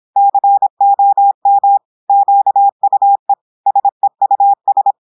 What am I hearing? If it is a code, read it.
Morse code, heard as a keyed tone, about 26 wpm, 800 Hz. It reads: COM QUE SEUS
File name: portugais texte en clair 2 vitesse 26 mots minutes